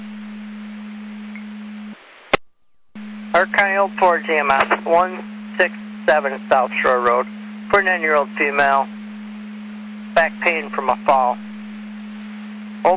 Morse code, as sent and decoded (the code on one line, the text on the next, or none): none